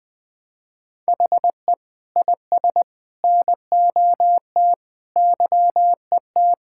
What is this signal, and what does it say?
Morse code, heard as a keyed tone, about 20 wpm, 700 Hz. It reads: HE IS NOT YET
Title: Sentences from Top 1000 Words - Review Part 6 - 20wpm